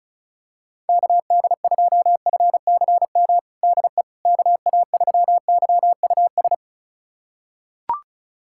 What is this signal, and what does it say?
Morse code, heard as a keyed tone, about 36 wpm, 700 Hz. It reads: KD2FCM DE KA3YUS
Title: USA Callsign DE Callsign - 6 Characters - 35wpm